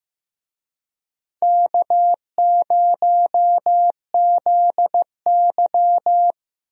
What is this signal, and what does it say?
Morse code, heard as a keyed tone, about 15 wpm, 700 Hz. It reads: K0ZY
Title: USA Call Signs - 4 Characters - Repeated 2x - 15wpm